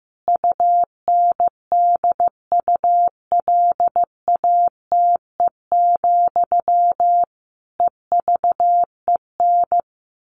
Morse code, heard as a keyed tone, about 15 wpm, 700 Hz. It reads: UNDULATE, EVEN
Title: A Sound Of Thunder 15wpm